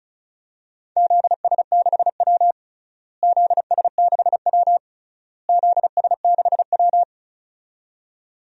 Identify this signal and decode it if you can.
Morse code, heard as a keyed tone, about 36 wpm, 700 Hz. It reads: ZS6W ZS6W ZS6W
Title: DXCC-100 Call Signs - 4 Characters - Repeated 3x - 35wpm